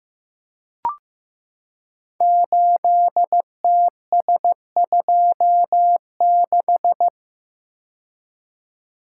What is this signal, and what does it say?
Morse code, heard as a keyed tone, about 15 wpm, 700 Hz. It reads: 8TS26
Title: Code Groups - 5 Letter:Numbers - 15wpm